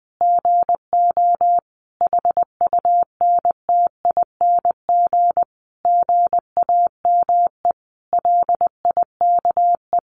Morse code, heard as a keyed tone, about 20 wpm, 700 Hz. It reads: GO HUNTING GAME LIKE
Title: A Sound Of Thunder 20wpm